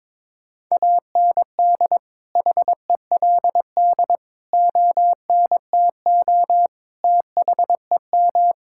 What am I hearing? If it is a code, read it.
Morse code, heard as a keyed tone, about 22 wpm, 700 Hz. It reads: AND HELD ONTO THEM
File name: Sentences with 9-Word Phrases - 22wpm